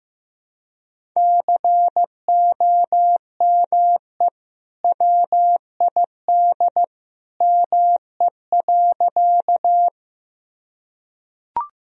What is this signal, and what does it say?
Morse code, heard as a keyed tone, about 15 wpm, 700 Hz. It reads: COME WID ME.
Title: Sentences from Top 200 Words and Abbreviations 15wpm